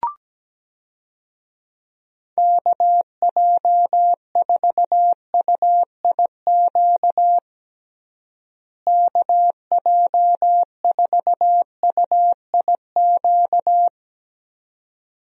Morse code, heard as a keyed tone, about 17 wpm, 700 Hz. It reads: KJ4UIQ KJ4UIQ
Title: USA Call Signs - 6 Characters - Repeated 2x - 17wpm